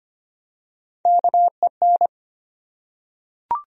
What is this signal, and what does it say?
Morse code, heard as a keyed tone, about 25 wpm, 700 Hz. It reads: KEN
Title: USA Common Names - Max 4 Letters - 25wpm